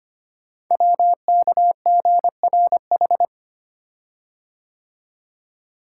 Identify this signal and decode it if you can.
Morse code, heard as a keyed tone, about 26 wpm, 700 Hz. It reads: WKGRH
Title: Code Groups - 5 Letters - 25wpm